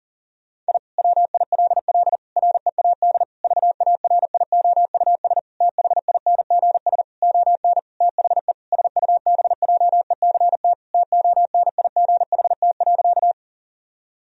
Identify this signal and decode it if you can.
Morse code, heard as a keyed tone, about 40 wpm, 700 Hz. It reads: I WILL READ VARIOUS THINGS ON THE SUBJECT TONIGHT.
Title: Sentences from Top 1000 Words - Review Part 4 - 40wpm